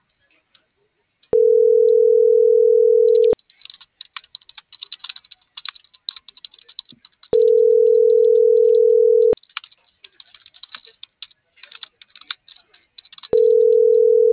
Ringback tone (North America), 440 + 480 Hz.